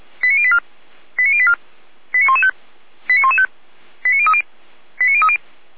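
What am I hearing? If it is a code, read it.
Five-tone selective call (ZVEI), heard as quick runs of five tones, tones of about 70 ms. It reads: 89084 89084 89115 89115 89030 89030
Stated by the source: Five-tone ZVEI selective call received by SDR.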